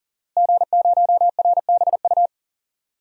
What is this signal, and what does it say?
Morse code, heard as a keyed tone, about 40 wpm, 700 Hz. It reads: G0RBU
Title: DXCC-100 Call Signs - 5 Characters - Repeated 3x - 40wpm